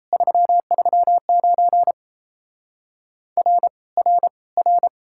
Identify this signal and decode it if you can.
Morse code, heard as a keyed tone, about 38 wpm, 700 Hz. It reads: 339 R R R